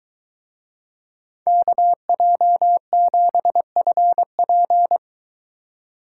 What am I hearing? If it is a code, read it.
Morse code, heard as a keyed tone, about 22 wpm, 700 Hz. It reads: KJ7FP